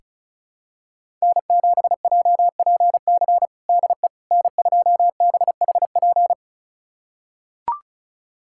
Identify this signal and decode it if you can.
Morse code, heard as a keyed tone, about 36 wpm, 700 Hz. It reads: N7JPC DE N2BHP
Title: USA Callsign DE Callsign - 5 Characters - 35wpm